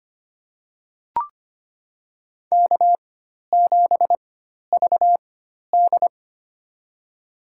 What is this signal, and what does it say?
Morse code, heard as a keyed tone, about 25 wpm, spaced as for 12 wpm, 700 Hz. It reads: K7VD
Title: USA Call Signs - 4 Characters - 4x Character Spacing - 25wpm